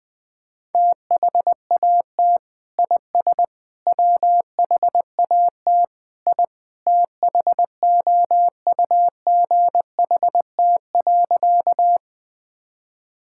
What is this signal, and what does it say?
Morse code, heard as a keyed tone, about 20 wpm, 700 Hz. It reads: THAT IS WHAT I THOUGHT.
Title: Sentences with 2-Word Phrases - 20wpm